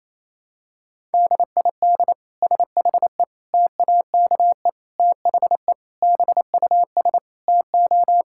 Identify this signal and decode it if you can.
Morse code, heard as a keyed tone, about 28 wpm, 700 Hz. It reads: DID SHE TAKE THE BUS TO
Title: Sentences from Top 1000 Words - Review Part 6 - 28wpm